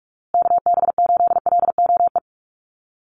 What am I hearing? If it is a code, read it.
Morse code, heard as a keyed tone, about 44 wpm, 700 Hz. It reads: KB8LOE
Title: USA Call Signs Repeated 2x 45wpm